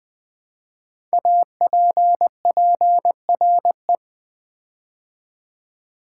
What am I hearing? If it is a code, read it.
Morse code, heard as a keyed tone, about 20 wpm, 700 Hz. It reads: APPRE
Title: Build A Word - 5 Letters At A Time - 10-Letter Words - 20wpm